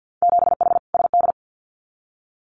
Morse code, heard as a keyed tone, about 50 wpm, 700 Hz. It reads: 7H SD
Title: Winter Field Day Exchanges Encore 50wpm